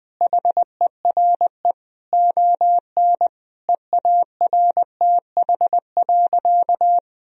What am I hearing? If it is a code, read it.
Morse code, heard as a keyed tone, about 21 wpm, 700 Hz. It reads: HERE ON EARTH.
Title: Sentences from Top 1000 Words - Review Part 2 - 20wpm